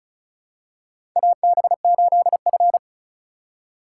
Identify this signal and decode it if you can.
Morse code, heard as a keyed tone, about 34 wpm, 700 Hz. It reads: AB8F
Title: USA Call Signs - 4 Characters - 6x Character Spacing - 35wpm